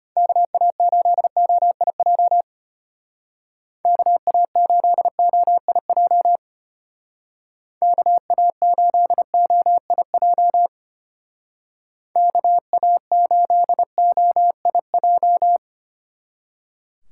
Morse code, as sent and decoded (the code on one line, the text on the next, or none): -.- .- ---.. --- .. .--- / -.- .- ---.. --- .. .--- / -.- .- ---.. --- .. .--- / -.- .- ---.. --- .. .---
KA8OIJ KA8OIJ KA8OIJ KA8OIJ